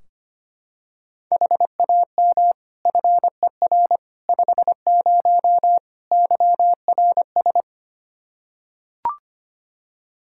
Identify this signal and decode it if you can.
Morse code, heard as a keyed tone, about 25 wpm, 700 Hz. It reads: HAM FER 50 YRS
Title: Practice QSO - Exchange 3 - More Info - 25wpm